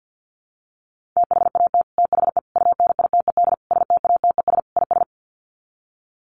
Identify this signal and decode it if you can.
Morse code, heard as a keyed tone, about 50 wpm, 700 Hz. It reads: THAT THE UNITED STATES IS
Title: Sets of 5 Words Encore 50wpm